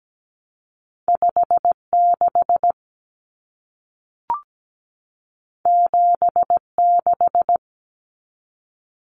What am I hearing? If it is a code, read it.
Morse code, heard as a keyed tone, about 17 wpm, 700 Hz. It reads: 5 6 7 6
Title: Common Cut Numbers - 2-Digit 17wpm